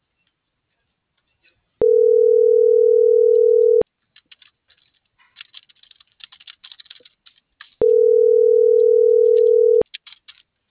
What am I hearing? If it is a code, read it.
Ringback tone (North America), 440 + 480 Hz.